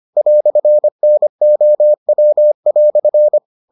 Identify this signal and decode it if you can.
Morse code, heard as a keyed tone, about 25 wpm, 600 Hz. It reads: "NOW"